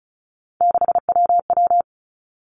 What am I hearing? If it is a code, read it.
Morse code, heard as a keyed tone, about 36 wpm, 700 Hz. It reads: 6WW
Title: Three Challenging Letter-Number Combinations 35wpm